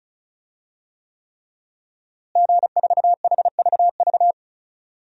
Morse code, heard as a keyed tone, about 34 wpm, 700 Hz. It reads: G4HVV
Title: DXCC-100 Call Signs - 5 Characters - Repeated 3x - 35wpm